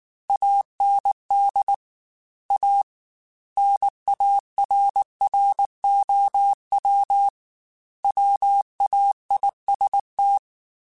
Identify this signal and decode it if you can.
Morse code, heard as a keyed tone, about 19 wpm, 800 Hz. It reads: AND A NARROW WAIST
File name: Chuckles / Chuckles-mp3s / MiddleAge @18